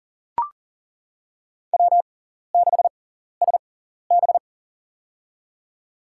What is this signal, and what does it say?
Morse code, heard as a keyed tone, about 40 wpm, spaced as for 14 wpm, 700 Hz. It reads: W6SB
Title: USA Call Signs - 4 Characters - 6x Character Spacing - 40wpm